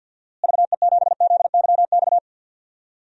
Morse code, heard as a keyed tone, about 46 wpm, 700 Hz. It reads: VE7ZYX
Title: DXCC-100 Call Signs - 6 Characters - Repeated 2x - 50wpm